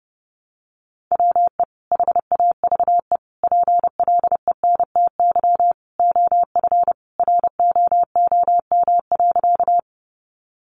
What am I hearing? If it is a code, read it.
Morse code, heard as a keyed tone, about 30 wpm, 700 Hz. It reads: WE HAVE PLENTY OF ROOM.
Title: Sentences from Top 1300 Words 30wpm